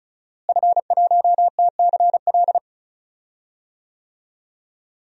Morse code, heard as a keyed tone, about 36 wpm, 700 Hz. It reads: F1TCL
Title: DXCC-100 Call Signs - 5 Characters - Repeated 3x - 35wpm